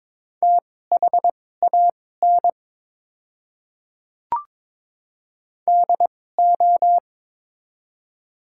Morse code, heard as a keyed tone, about 22 wpm, spaced as for 16 wpm, 700 Hz. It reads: THAN DO
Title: 2x Character Spacing - Top 100 Words - 22wpm